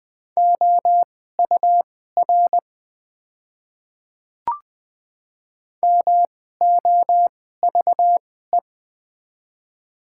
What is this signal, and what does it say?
Morse code, heard as a keyed tone, about 21 wpm, spaced as for 15 wpm, 700 Hz. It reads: OUR MOVE
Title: 2x Character Spacing - Top 200 Words - 20wpm